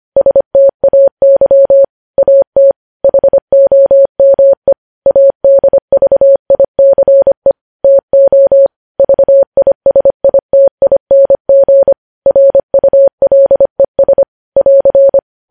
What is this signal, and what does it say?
Morse code, heard as a keyed tone, about 25 wpm, 550 Hz. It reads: STAY AT HOME ADVICE TO VISITING RULES <AR>
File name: podcast / News headlines-1x-25x25